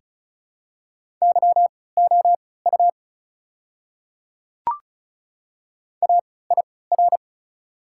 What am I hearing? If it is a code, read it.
Morse code, heard as a keyed tone, about 36 wpm, spaced as for 20 wpm, 700 Hz. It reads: YOU AIR
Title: 3x Character Spacing - Top 200 Words - 35wpm